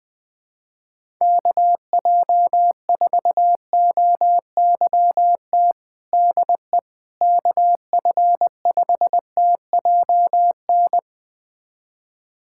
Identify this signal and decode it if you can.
Morse code, heard as a keyed tone, about 20 wpm, 700 Hz. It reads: KJ4OYT DE KF5TJN
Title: USA Callsign DE Callsign - 6 Characters - 20wpm